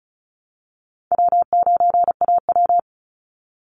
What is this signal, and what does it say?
Morse code, heard as a keyed tone, about 34 wpm, 700 Hz. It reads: W9AW
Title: Call Signs Repeated 3x - 35wpm